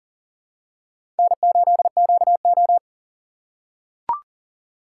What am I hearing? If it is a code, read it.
Morse code, heard as a keyed tone, about 40 wpm, 690 Hz. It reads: N8QO
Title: USA Call Signs - 6x Character Spacing - 40wpm